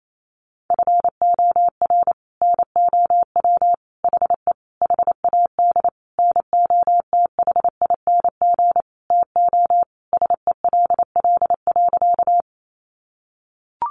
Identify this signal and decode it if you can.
Morse code, heard as a keyed tone, about 28 wpm, 700 Hz. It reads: FOR NOW HE HAD NOTHING TO SELL.
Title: Sentences from Top 1100 Words 28wpm